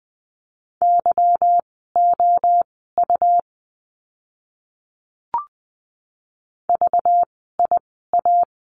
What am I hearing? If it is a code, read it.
Morse code, heard as a keyed tone, about 20 wpm, spaced as for 15 wpm, 700 Hz. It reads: YOU VIA
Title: 192 Lesson 31f - Letter v - Words 20wpm